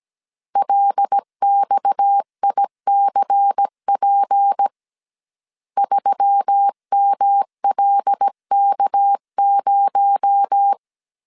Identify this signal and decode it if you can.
Morse code, heard as a keyed tone, about 17 wpm, 800 Hz. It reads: LXICP 3MLK0
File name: CW / DET / 1 6 1500C S A 16wpm 2635